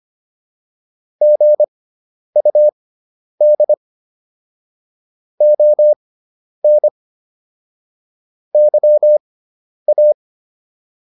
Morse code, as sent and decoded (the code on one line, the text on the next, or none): --. ..- -.. / --- -. / -.-- .-
GUD ON YA